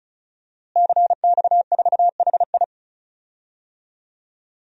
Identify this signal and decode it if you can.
Morse code, heard as a keyed tone, about 36 wpm, 700 Hz. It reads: CX4HI
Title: DXCC-100 Call Signs - 5 Characters - Repeated 2x - 35wpm